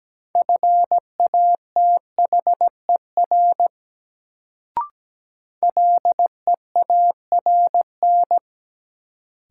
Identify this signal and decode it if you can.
Morse code, heard as a keyed tone, about 17 wpm, 700 Hz. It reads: FATHER LEARN
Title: Sentences from Top 300 Words - New Word Drill - 17wpm